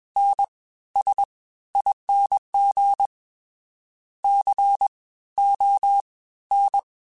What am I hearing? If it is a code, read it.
Morse code, heard as a keyed tone, about 21 wpm, 800 Hz. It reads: N S ING C O N
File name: Special INGs-mp3s / Special INGs-mp3s / Fast ing @20